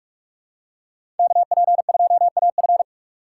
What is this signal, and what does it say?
Morse code, heard as a keyed tone, about 46 wpm, 700 Hz. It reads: KP2AF
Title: USA Call Signs - 5 Characters - 6x Character Spacing - 45wpm